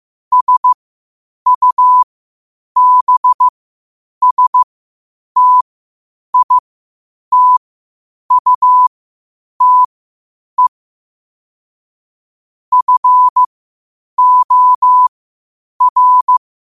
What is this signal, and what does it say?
Morse code, heard as a keyed tone, about 15 wpm, spaced as for 8 wpm, 1000 Hz. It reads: SUBSTITUTE FOR
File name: qotdfiles / 8wpm